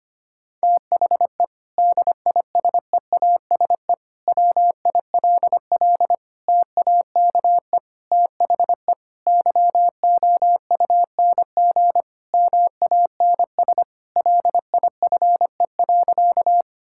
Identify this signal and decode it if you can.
Morse code, heard as a keyed tone, about 25 wpm, 700 Hz. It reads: THE DISEASE WILL TAKE THE YOUNG MANS LIFE.
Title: Sentences from Top 1000 Words - Review Part 6 - 25wpm